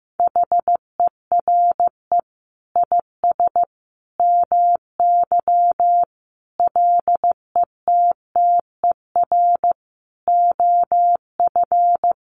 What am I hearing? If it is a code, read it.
Morse code, heard as a keyed tone, about 15 wpm, 700 Hz. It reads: HERE IS MY LETTER OF
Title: Sentences from Top 1400 Words 15wpm